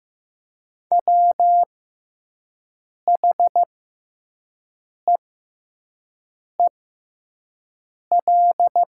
Morse code, heard as a keyed tone, about 15 wpm, spaced as for 5 wpm, 700 Hz. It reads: WHEEL
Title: 6x Character Spacing - Top 500 Words - 15wpm